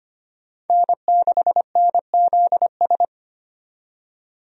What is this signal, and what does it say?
Morse code, heard as a keyed tone, about 25 wpm, 700 Hz. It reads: N6NZS
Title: USA Call Signs - 5 Characters - 25wpm